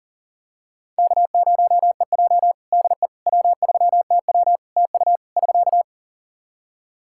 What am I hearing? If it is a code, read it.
Morse code, heard as a keyed tone, about 40 wpm, 700 Hz. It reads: K0EJ DE W3TW TU <SK>
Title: Practice QSO - Exchange 4 - Ending - 40wpm